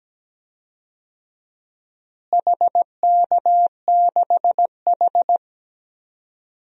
Morse code, heard as a keyed tone, about 17 wpm, 700 Hz. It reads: HK6H
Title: DXCC-100 Call Signs - 4 Characters - Repeated 2x - 17wpm